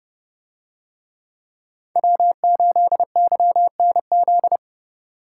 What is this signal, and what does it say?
Morse code, heard as a keyed tone, about 28 wpm, 700 Hz. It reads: W8YNZ